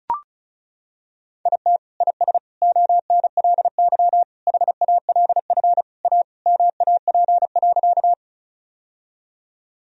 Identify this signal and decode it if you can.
Morse code, heard as a keyed tone, about 36 wpm, 700 Hz. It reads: IT IS ONLY HALF A MAP.
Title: Sentences from Top 1000 Words - Review Part 2 - 35wpm